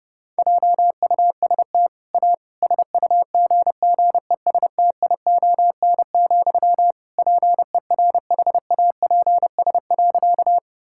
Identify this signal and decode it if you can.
Morse code, heard as a keyed tone, about 30 wpm, 700 Hz. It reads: JUST A SUGGESTION, PERHAPS.
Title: Sentences from Top 1600 Words 30wpm